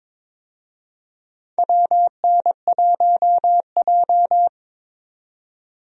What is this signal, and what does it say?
Morse code, heard as a keyed tone, about 22 wpm, 700 Hz. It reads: WN1J
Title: USA Call Signs - 4 Characters - 22wpm